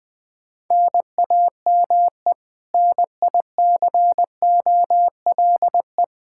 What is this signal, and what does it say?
Morse code, heard as a keyed tone, about 20 wpm, 700 Hz. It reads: NAME NICOLE
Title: Practice QSO - Exchange 1 - Beginning - 20wpm